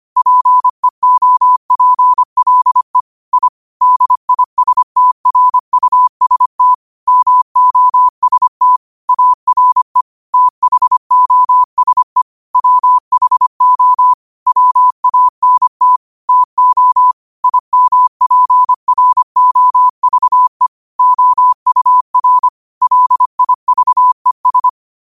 Morse code, heard as a keyed tone, about 25 wpm, 1000 Hz. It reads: PEOPLE I DISTRUST MOST ARE THOSE WHO WANT TO IMPROVE OUR LIVES
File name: qotdfiles / 25wpm